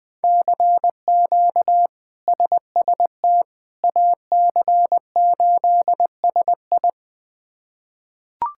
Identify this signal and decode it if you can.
Morse code, heard as a keyed tone, about 20 wpm, 700 Hz. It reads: CQ SST AC8SI
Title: K1USN SST QSO Practice - 20wpm